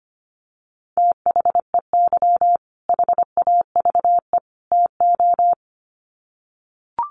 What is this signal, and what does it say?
Morse code, heard as a keyed tone, about 25 wpm, 700 Hz. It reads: THEY HAVE TO
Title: Sets of 3 Words 25wpm